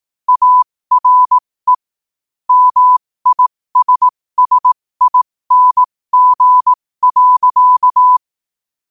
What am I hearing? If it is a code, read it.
Morse code, heard as a keyed tone, about 18 wpm, spaced as for 15 wpm, 1000 Hz. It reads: ARE MISSING.
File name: qotdfiles / 15wpm